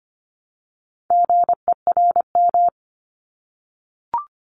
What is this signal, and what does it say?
Morse code, heard as a keyed tone, about 25 wpm, 700 Hz. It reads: GERM
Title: Multi-letter Phonemes Repeated 3x 25wpm